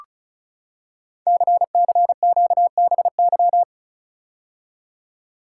Morse code, heard as a keyed tone, about 35 wpm, 700 Hz. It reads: CCQBY
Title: Code Groups - 5 Letters - 35wpm